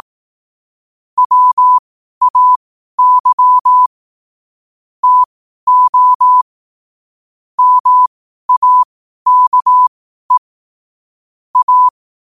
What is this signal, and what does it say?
Morse code, heard as a keyed tone, about 18 wpm, spaced as for 12 wpm, 1000 Hz. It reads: WAY TO MAKE A
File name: qotdfiles / 12wpm